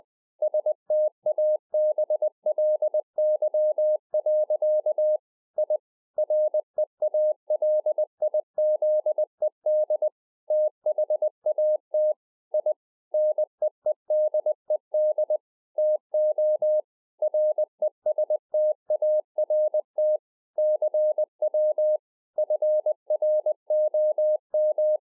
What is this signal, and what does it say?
Morse code, heard as a keyed tone, about 20 wpm, 610 Hz. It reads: STABLY. I REALIZED THAT I NEEDED TO RESTART CW FROM